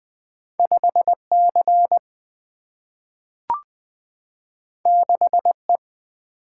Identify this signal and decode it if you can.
Morse code, heard as a keyed tone, about 20 wpm, 700 Hz. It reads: 5C 6E
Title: Single-Letter Single-Number 20wpm